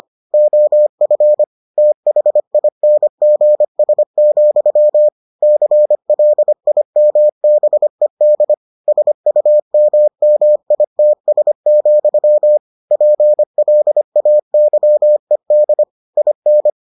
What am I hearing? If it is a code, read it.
Morse code, heard as a keyed tone, about 25 wpm, 600 Hz. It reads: OF THINGS, CLIMBED SUMMITS, PLAYED IN